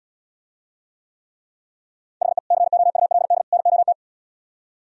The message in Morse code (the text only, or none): HE COULD NOT